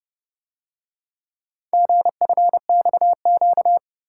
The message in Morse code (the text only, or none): GFXQ